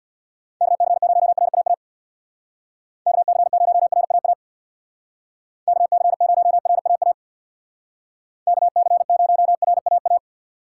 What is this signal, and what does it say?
Morse code, heard as a keyed tone, about 66 wpm, 700 Hz. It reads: KC0RAA KC0RAA KC0RAA KC0RAA